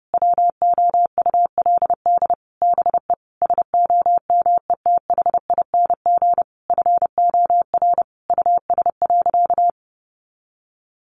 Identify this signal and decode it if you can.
Morse code, heard as a keyed tone, about 30 wpm, 700 Hz. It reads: WOULD BE SOMETHING FOR US.
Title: Sentences from Top 1200 Words 30wpm